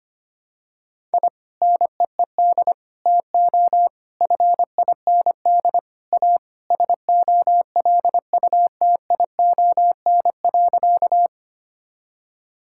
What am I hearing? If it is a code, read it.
Morse code, heard as a keyed tone, about 25 wpm, 700 Hz. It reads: I NEED TO FIND A SOLUTION.
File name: practice / 25wpm / Sentences with 3-Word Phrases - 25wpm